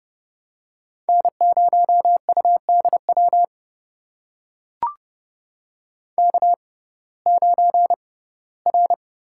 Morse code, heard as a keyed tone, about 30 wpm, spaced as for 18 wpm, 700 Hz. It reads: N0UDW K9R
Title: USA Call Signs - 5 Characters - 6x Character Spacing - 30wpm